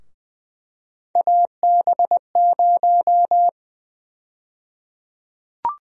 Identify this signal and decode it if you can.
Morse code, heard as a keyed tone, about 20 wpm, 700 Hz. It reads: AB0
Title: USA Call Sign Prefixes - Repeated 2x - 20wpm